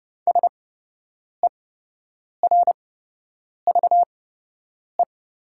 Morse code, heard as a keyed tone, about 30 wpm, spaced as for 8 wpm, 690 Hz. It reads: SERVE
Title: 8x Character Spacing - Top 500 Words - 30wpm